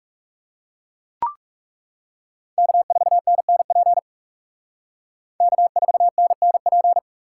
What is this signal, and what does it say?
Morse code, heard as a keyed tone, about 37 wpm, 700 Hz. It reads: K4NNP K4NNP
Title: Speed Racing - USA 1x3 Call Signs 30wpm